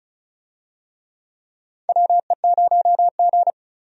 Morse code, heard as a keyed tone, about 36 wpm, 700 Hz. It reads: WE0G
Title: USA Call Signs - 4 Characters - 35wpm